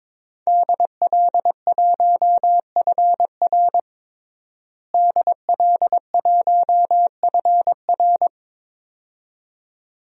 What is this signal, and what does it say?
Morse code, heard as a keyed tone, about 22 wpm, 700 Hz. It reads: DL1FR DL1FR
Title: DXCC-100 Call Signs - 5 Characters - Repeated 2x - 22wpm